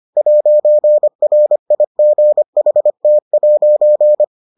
Morse code, heard as a keyed tone, about 25 wpm, 600 Hz. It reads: 'RIGHT'